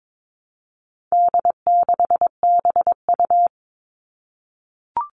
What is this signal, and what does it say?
Morse code, heard as a keyed tone, about 22 wpm, 700 Hz. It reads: D6BU
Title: All Cut Numbers - 4-Digit 22wpm